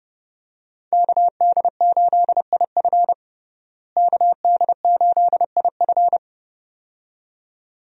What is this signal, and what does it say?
Morse code, heard as a keyed tone, about 30 wpm, 700 Hz. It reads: KD8IF KD8IF
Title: USA Call Signs - 5 Characters - Repeated 2x - 30wpm